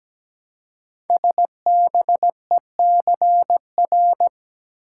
Morse code, heard as a keyed tone, about 17 wpm, 700 Hz. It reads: SBECR
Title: Code Groups - 5 Letters - 17wpm